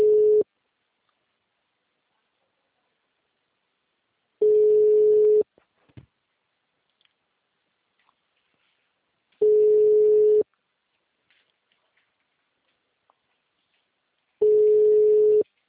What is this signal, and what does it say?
Ringback tone (Europe, 425 Hz), with speech.